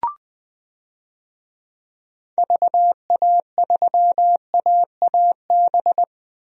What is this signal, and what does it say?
Morse code, heard as a keyed tone, about 20 wpm, 700 Hz. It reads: VA3AAB
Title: DXCC-100 Call Signs - 6 Characters - Repeated 3x - 20wpm